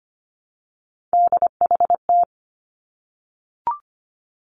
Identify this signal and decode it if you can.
Morse code, heard as a keyed tone, about 25 wpm, 700 Hz. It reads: DHT
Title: Three Challenging Letter-Number Combinations 25wpm